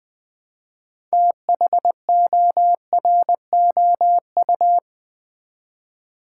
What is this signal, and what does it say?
Morse code, heard as a keyed tone, about 20 wpm, 700 Hz. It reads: THOROU
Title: Build A Word - 3 Letters At A Time - 8-Letter Words - 20wpm